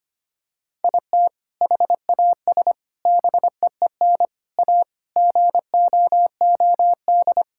Morse code, heard as a keyed tone, about 25 wpm, 700 Hz. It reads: IT HAS BEEN A GOOD
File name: practice / 25wpm / Sentences with 2-Word Phrases - 25wpm